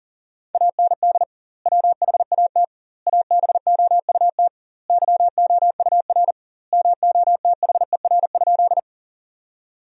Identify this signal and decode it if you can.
Morse code, heard as a keyed tone, about 40 wpm, 700 Hz. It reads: AND WHAT ABOUT YOUR MOTHER?
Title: 2x Word-Spacing - Sentences from Top 200 Words - 40wpm